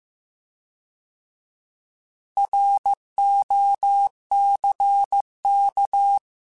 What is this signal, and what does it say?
Morse code, heard as a keyed tone, about 15 wpm, 800 Hz. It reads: ROCK
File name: fromSMs / FromSMs mp3s / Cities@13